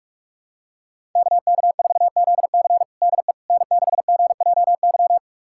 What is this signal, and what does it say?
Morse code, heard as a keyed tone, about 46 wpm, 700 Hz. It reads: KK4ZC DE N6GJY
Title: USA Callsign DE Callsign - 5 Characters - 45wpm